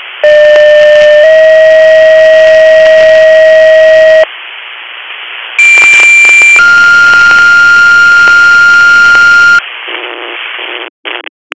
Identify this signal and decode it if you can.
Two-tone page, heard as a one-second tone, then a three-second tone, about 600 then 640 Hz, about 2350 then 1400 Hz, with noise. It is heard from 0.2 to 9.6 s.